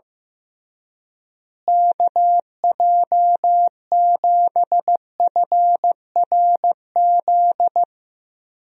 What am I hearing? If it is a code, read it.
Morse code, heard as a keyed tone, about 15 wpm, 700 Hz. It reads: KJ7FRZ
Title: USA Call Signs - 6 Characters - Repeated 3x - 15wpm